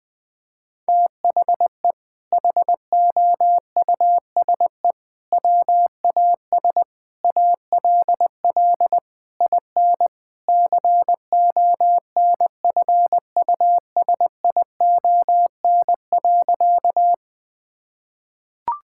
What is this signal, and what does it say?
Morse code, heard as a keyed tone, about 21 wpm, 700 Hz. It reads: THE HOUSE WAS ALL IN CONFUSION.
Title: Sentences from Top 1600 Words 20wpm